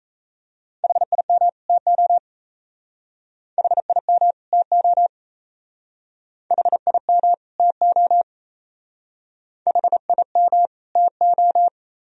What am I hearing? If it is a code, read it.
Morse code, heard as a keyed tone, about 34 wpm, 700 Hz. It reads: HIM TO HIM TO HIM TO HIM TO
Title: Speed-Racing - Sets of 2 Words 28wpm